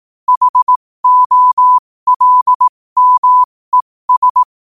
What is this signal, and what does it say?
Morse code, heard as a keyed tone, about 18 wpm, spaced as for 15 wpm, 1000 Hz. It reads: HOLMES
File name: qotdfiles / 15wpm